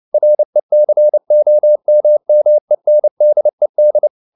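Morse code, heard as a keyed tone, about 29 wpm, 600 Hz. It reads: RECOMMENDED
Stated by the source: Morse code at 29 wpm, sending Ken Follett's novel Never in English, from CW med Gnister.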